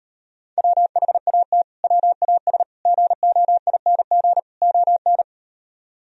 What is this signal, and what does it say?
Morse code, heard as a keyed tone, about 38 wpm, 700 Hz. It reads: WHAT WAS GOING ON